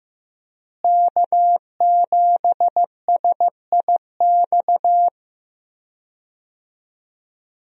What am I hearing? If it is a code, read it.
Morse code, heard as a keyed tone, about 15 wpm, 700 Hz. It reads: K7SIX
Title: USA Call Signs - 5 Characters - Repeated 2x - 15wpm